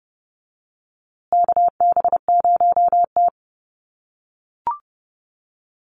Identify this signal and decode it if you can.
Morse code, heard as a keyed tone, about 28 wpm, 700 Hz. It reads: KB0T
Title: USA Call Signs Repeated 3x 30wpm